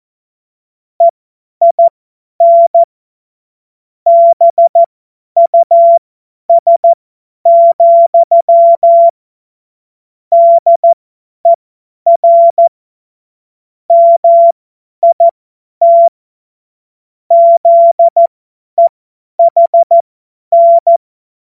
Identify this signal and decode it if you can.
Morse code, heard as a keyed tone, about 14 wpm, spaced as for 10 wpm, 680 Hz. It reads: EIN BUS, DER MIT ZEHN
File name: Morsekurs / Witze / Witz 26. April 2022 50 F70